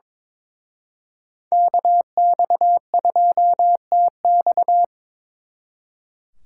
Morse code, heard as a keyed tone, about 26 wpm, 700 Hz. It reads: KX2TX
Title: Speed Racing - USA 2x2 Call Signs 22wpm